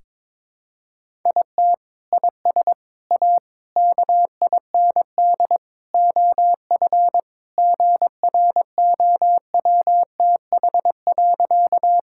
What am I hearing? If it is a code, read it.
Morse code, heard as a keyed tone, about 22 wpm, 700 Hz. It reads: IT IS A KIND OF GROWTH.
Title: Sentences from Top 1000 Words - Review Part 5 - 22wpm